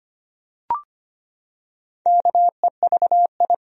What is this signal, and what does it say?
Morse code, heard as a keyed tone, about 25 wpm, 700 Hz. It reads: KEVI
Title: USA Common Names - Max 5 Letters - 25wpm